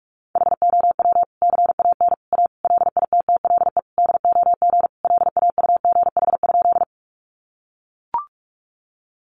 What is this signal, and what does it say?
Morse code, heard as a keyed tone, about 44 wpm, 700 Hz. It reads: HOW CAN A LITTLE DOG LAUGH?
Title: Sentences from Top 500 Words 45wpm